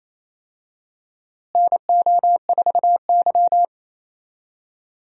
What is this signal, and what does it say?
Morse code, heard as a keyed tone, about 27 wpm, 700 Hz. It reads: NO4Y
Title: USA Call Signs - 4 Characters - Repeated 2x - 28wpm